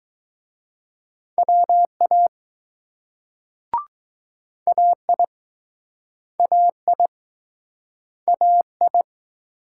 Morse code, heard as a keyed tone, about 20 wpm, 700 Hz. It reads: WA AI AI AI